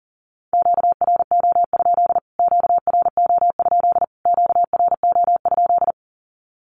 Morse code, heard as a keyed tone, about 40 wpm, 700 Hz. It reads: QRO? QRO? QRO?
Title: Q-Codes Repeated 3x 40wpm